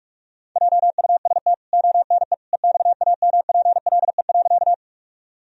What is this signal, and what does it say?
Morse code, heard as a keyed tone, about 44 wpm, 700 Hz. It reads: JUST ONE EXAMPLE.
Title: Sentences from Top 1000 Words - Review Part 1 - 45wpm